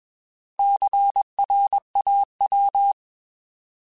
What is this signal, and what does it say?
Morse code, heard as a keyed tone, about 20 wpm, 800 Hz. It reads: CRAW